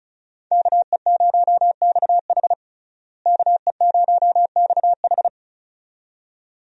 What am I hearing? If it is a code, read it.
Morse code, heard as a keyed tone, about 35 wpm, 700 Hz. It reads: KE0XH KE0XH
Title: USA Call Signs - 5 Characters - Repeated 2x - 35wpm